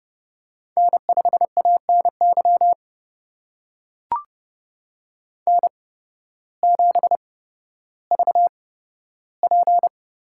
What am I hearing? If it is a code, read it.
Morse code, heard as a keyed tone, about 29 wpm, spaced as for 14 wpm, 700 Hz. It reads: N5ANY N7VP
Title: USA Call Signs - 5 Characters - 8x Character Spacing - 30wpm